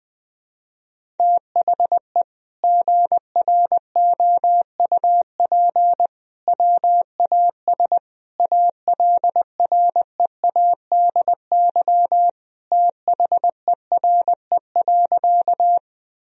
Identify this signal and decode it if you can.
Morse code, heard as a keyed tone, about 20 wpm, 700 Hz. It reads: THE GROUP WAS ALREADY THERE.
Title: Sentences from Top 1000 Words - Review Part 1 - 20wpm